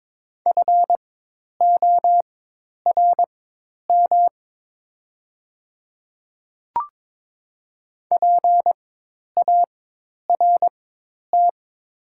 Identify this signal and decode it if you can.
Morse code, heard as a keyed tone, about 23 wpm, spaced as for 10 wpm, 700 Hz. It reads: FORM PART
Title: 4x Character Spacing - Top 300 Words - 22wpm